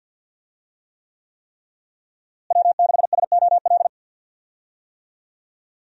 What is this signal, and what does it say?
Morse code, heard as a keyed tone, about 46 wpm, 700 Hz. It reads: W6SOL
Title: USA Call Signs - 5 Characters - Repeated 3x - 50wpm